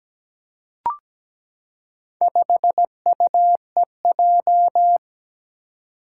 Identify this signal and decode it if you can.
Morse code, heard as a keyed tone, about 17 wpm, 700 Hz. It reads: 5UEJ
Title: Four Challenging Letter-Number Combinations 17wpm